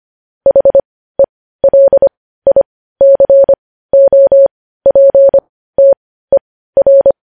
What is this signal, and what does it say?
Morse code, heard as a keyed tone, about 25 wpm, spaced as for 15 wpm, 550 Hz. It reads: HELICOPTER
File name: podcast / News headlines-1x-25x15